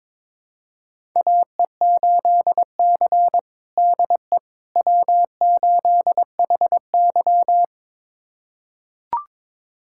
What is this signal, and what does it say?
Morse code, heard as a keyed tone, about 22 wpm, 700 Hz. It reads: AE8C DE W8HY
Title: USA Callsign DE Callsign - 4 Characters - 22wpm